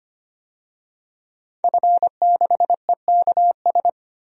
Morse code, heard as a keyed tone, about 25 wpm, 690 Hz. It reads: F6EKS
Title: DXCC-100 Call Signs - 5 Characters - 25wpm